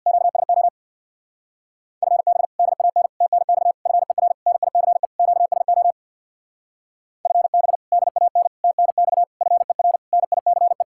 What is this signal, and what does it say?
Morse code, heard as a keyed tone, about 60 wpm, 700 Hz. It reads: QSO FB DAN TNX FER NICE QSO FB DAN TNX FER NICE
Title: Speed-Racing Common QSO Phrases - 50wpm